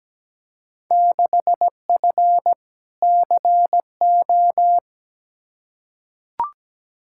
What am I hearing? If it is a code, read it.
Morse code, heard as a keyed tone, about 17 wpm, 700 Hz. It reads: 6F CO
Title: Field Day Exchanges Encore - 17wpm